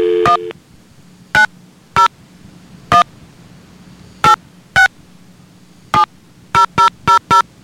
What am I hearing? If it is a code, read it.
DTMF, heard as telephone touch tones, tones of about 120 ms. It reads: 1901 8670000